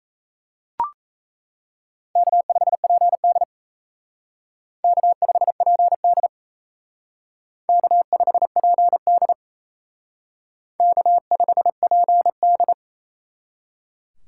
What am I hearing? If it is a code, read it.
Morse code, heard as a keyed tone, about 35 wpm, 700 Hz. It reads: K5PD K5PD K5PD K5PD
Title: Speed Racing - USA Call Signs 28wpm